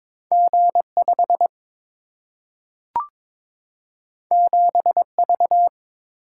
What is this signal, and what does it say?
Morse code, heard as a keyed tone, about 22 wpm, 700 Hz. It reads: G5 7V
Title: Single-Letter Single-Number 22wpm